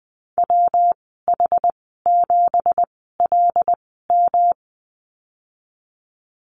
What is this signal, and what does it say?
Morse code, heard as a keyed tone, about 20 wpm, spaced as for 15 wpm, 700 Hz. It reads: WH7LM
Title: 193 Lesson 31g - Letter v - Call Signs 20wpm